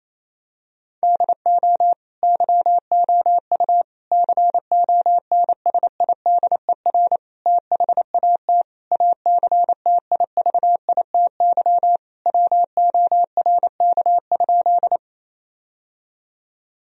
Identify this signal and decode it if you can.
Morse code, heard as a keyed tone, about 28 wpm, 700 Hz. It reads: DO YOU CONSIDER THAT ACTIVITY WORK?
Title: Sentences from Top 1000 Words - Review Part 4 - 28wpm